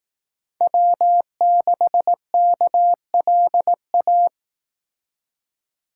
Morse code, heard as a keyed tone, about 18 wpm, 700 Hz. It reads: W6KLA
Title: Speed Racing - USA 1x3 Call Signs 15wpm